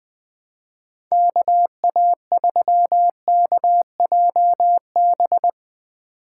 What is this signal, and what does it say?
Morse code, heard as a keyed tone, about 20 wpm, 700 Hz. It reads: KA3KJB